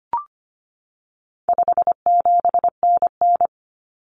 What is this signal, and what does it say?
Morse code, heard as a keyed tone, about 25 wpm, 700 Hz. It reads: 57NN
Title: Common Cut Numbers - 4-Digit 25wpm